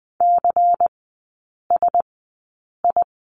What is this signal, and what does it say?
Morse code, heard as a keyed tone, about 20 wpm, spaced as for 8 wpm, 700 Hz. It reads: CSI
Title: 093 Lesson 16e - Letter c - Three Characters 20wpm